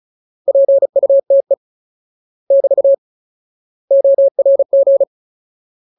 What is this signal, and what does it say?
Morse code, heard as a keyed tone, about 34 wpm, 550 Hz. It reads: PUTE <BT> ORG
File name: Ultimate Word List / 35WPM Ultimate